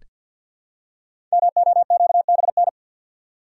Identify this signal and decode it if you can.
Morse code, heard as a keyed tone, about 46 wpm, 700 Hz. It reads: MOQBN
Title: Code Groups - 5 Letters - 50wpm